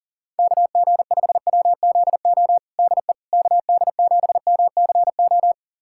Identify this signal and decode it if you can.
Morse code, heard as a keyed tone, about 40 wpm, 700 Hz. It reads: KG5WZO DE KD7MCO
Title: USA Callsign DE Callsign 40wpm